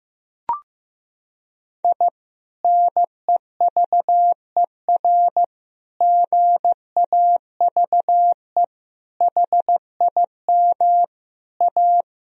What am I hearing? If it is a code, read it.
Morse code, heard as a keyed tone, about 15 wpm, 700 Hz. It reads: I NEVER GAVE HIM A
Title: Sentences from Top 1000 Words - Review Part 1 - 15wpm